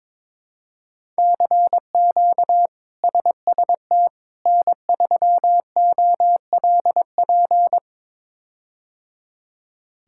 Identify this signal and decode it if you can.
Morse code, heard as a keyed tone, about 22 wpm, 700 Hz. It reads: CQ SST N3OLP
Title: K1USN SST QSO Practice - 22wpm